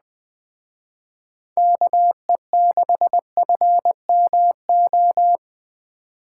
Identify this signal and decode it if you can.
Morse code, heard as a keyed tone, about 20 wpm, 700 Hz. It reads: KE6FMO
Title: Speed Racing - USA Call Signs 17wpm